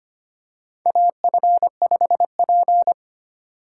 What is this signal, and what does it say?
Morse code, heard as a keyed tone, about 25 wpm, 700 Hz. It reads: AF5P
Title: K1USN SST QSO Practice - 25wpm